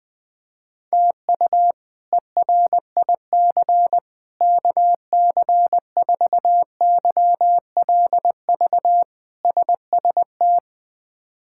Morse code, heard as a keyed tone, about 21 wpm, 700 Hz. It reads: TU ERIC KC4YLV SST
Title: K1USN SST QSO Practice - 20wpm